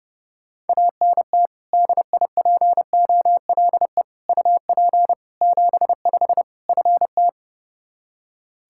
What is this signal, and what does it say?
Morse code, heard as a keyed tone, about 32 wpm, 700 Hz. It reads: ANT DIPOLE UP 75 FT
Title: Practice QSO - Exchange 2 - Info - 30wpm